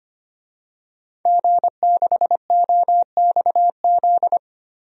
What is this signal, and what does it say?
Morse code, heard as a keyed tone, about 25 wpm, 700 Hz. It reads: G6OXZ
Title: DXCC-100 Call Signs - 5 Characters - Repeated 3x - 25wpm